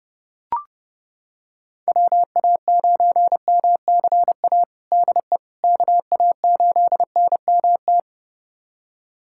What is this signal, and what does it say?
Morse code, heard as a keyed tone, about 30 wpm, 700 Hz. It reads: WA9MCA DE KA8NMT
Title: USA Callsign DE Callsign - 6 Characters - 30wpm